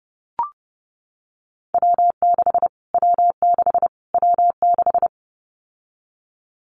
Morse code, heard as a keyed tone, about 30 wpm, 690 Hz. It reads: W6 W6 W6
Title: Callsign Prefixes for Top 25 DXCC - Repeated 3x - 30wpm